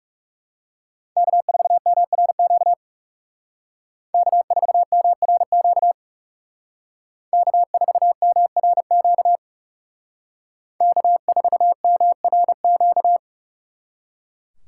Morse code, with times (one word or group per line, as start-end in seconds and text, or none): 1.2-2.7 K4MRQ
4.1-5.9 K4MRQ
7.3-9.4 K4MRQ
10.8-13.2 K4MRQ